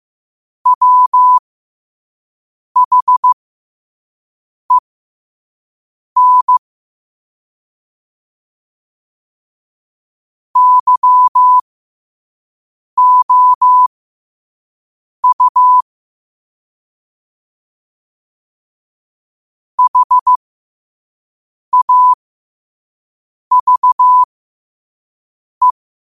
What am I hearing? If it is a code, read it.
Morse code, heard as a keyed tone, about 15 wpm, spaced as for 5 wpm, 1000 Hz. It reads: WHEN YOU HAVE